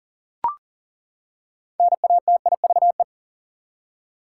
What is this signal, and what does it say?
Morse code, heard as a keyed tone, about 40 wpm, 700 Hz. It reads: NATIVE
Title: Six Letter Words - 40wpm